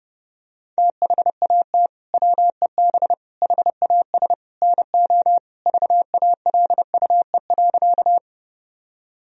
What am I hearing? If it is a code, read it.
Morse code, heard as a keyed tone, about 32 wpm, 700 Hz. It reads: THAT WEB HAS NO VALUE.
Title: Sentences from Top 1000 Words - Review Part 4 - 30wpm